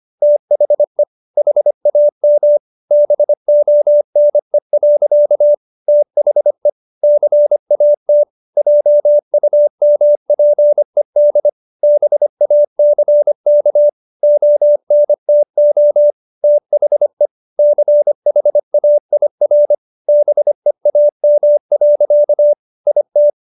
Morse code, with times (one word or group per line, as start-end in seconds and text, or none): 0.2-1.0 THE
1.4-2.6 HAM
2.9-5.5 BONE.
5.9-6.7 THE
7.0-8.2 CAT
8.6-11.5 JUMPED
11.8-13.9 BACK
14.2-16.1 ONTO
16.4-17.3 THE
17.6-19.7 CHAIR
20.1-22.5 BEAM.
22.9-23.3 IT